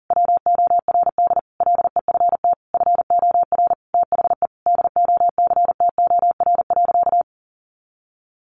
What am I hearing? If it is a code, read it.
Morse code, heard as a keyed tone, about 37 wpm, 700 Hz. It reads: WORD LEFT FOR THE DOCTOR.
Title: Sentences from Top 1500 Words 40wpm